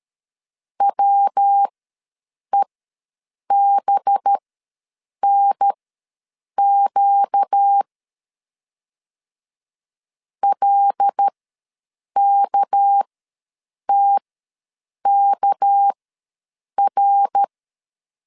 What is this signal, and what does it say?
Morse code, heard as a keyed tone, about 13 wpm, spaced as for 7 wpm, 800 Hz. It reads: WEBNQ LKTKR
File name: LEZIONI / 6 PKX0 / 6 PK ws 12wpm 3116